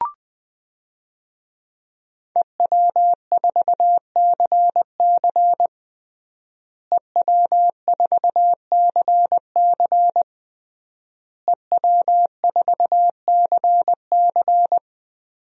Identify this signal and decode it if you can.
Morse code, heard as a keyed tone, about 20 wpm, 700 Hz. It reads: EW4CC EW4CC EW4CC
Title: DXCC-100 Call Signs - 5 Characters - Repeated 3x - 20wpm